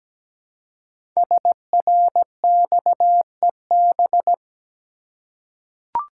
Morse code, heard as a keyed tone, about 17 wpm, 700 Hz. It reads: SRXEB
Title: Code Groups - 5 Letters - 17wpm